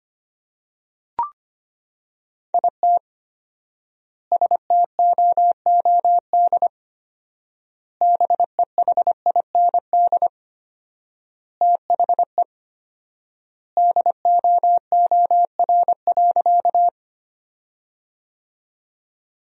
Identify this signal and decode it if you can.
Morse code, heard as a keyed tone, about 25 wpm, 700 Hz. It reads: IT STOOD BEHIND THE DOOR.
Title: 4x Word-Spacing - Sentences from Top 500 Words - 25wpm